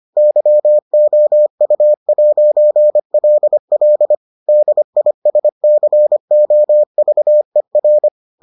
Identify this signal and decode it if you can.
Morse code, heard as a keyed tone, about 25 wpm, 600 Hz. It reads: YOU'LL DISCOVER